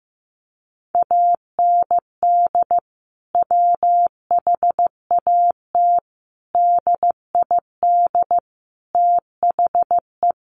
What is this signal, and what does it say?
Morse code, heard as a keyed tone, about 15 wpm, 700 Hz. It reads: AND WHAT DID THE
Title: Sentences from Top 1500 Words 15wpm